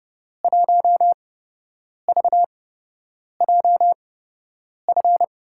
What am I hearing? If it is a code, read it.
Morse code, heard as a keyed tone, about 30 wpm, spaced as for 8 wpm, 700 Hz. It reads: 1VJF
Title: USA Call Signs - 6 Characters - 8x Character Spacing - 30wpm